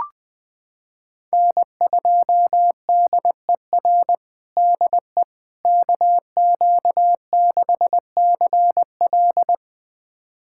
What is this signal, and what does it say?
Morse code, heard as a keyed tone, about 21 wpm, 700 Hz. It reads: N2DER DE KQ6CL
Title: USA Callsign DE Callsign - 5 Characters - 20wpm